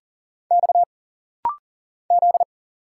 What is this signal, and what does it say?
Morse code, heard as a keyed tone, about 40 wpm, 700 Hz. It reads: X Z
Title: Single Letters - Rapid-Fire 40wpm